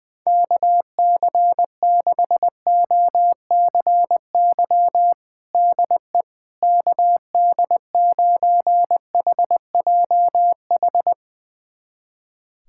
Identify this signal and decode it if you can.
Morse code, heard as a keyed tone, about 20 wpm, 700 Hz. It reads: KC6OCY DE KD9HJH
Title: USA Callsign DE Callsign - 6 Characters - 20wpm